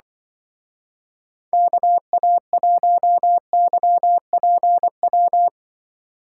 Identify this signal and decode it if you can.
Morse code, heard as a keyed tone, about 24 wpm, 700 Hz. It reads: KA1YPW
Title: Speed Racing - USA 2x3 Call Signs 20wpm